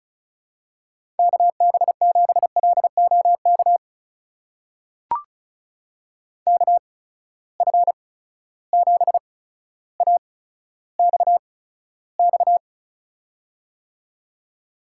Morse code, heard as a keyed tone, about 36 wpm, spaced as for 15 wpm, 700 Hz. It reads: KB7LOK KF7AXX
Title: USA Call Signs - 6 Characters - 8x Character Spacing - 35wpm